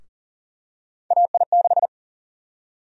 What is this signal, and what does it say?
Morse code, heard as a keyed tone, about 40 wpm, 700 Hz. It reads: AI6
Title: USA Call Sign Prefixes - 40wpm